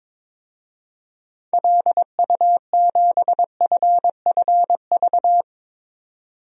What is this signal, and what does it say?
Morse code, heard as a keyed tone, about 22 wpm, 700 Hz. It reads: LU7FFV
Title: DXCC-100 Call Signs - 6 Characters - 22wpm